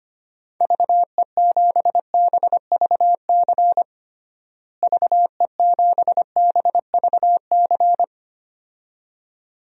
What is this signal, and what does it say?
Morse code, heard as a keyed tone, about 25 wpm, 700 Hz. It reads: VE7BVC VE7BVC
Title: DXCC-100 Call Signs - 6 Characters - Repeated 2x - 25wpm